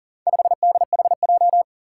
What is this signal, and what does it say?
Morse code, heard as a keyed tone, about 40 wpm, 700 Hz. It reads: 5DHJ
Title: Four Challenging Letter-Number Combinations 40wpm